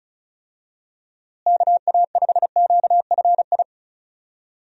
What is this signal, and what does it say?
Morse code, heard as a keyed tone, about 36 wpm, 700 Hz. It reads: KA5QFI
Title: USA Call Signs - 6 Characters - Repeated 3x - 35wpm